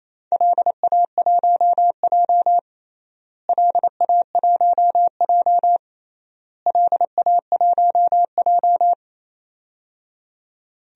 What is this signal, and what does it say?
Morse code, heard as a keyed tone, about 28 wpm, 700 Hz. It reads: LA1J LA1J LA1J
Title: DXCC-100 Call Signs - 4 Characters - Repeated 3x - 28wpm